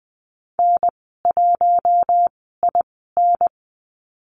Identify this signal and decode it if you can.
Morse code, heard as a keyed tone, about 20 wpm, spaced as for 15 wpm, 700 Hz. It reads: N1IN
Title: 048 Lesson 9g - Number 4 - Call Signs 20wpm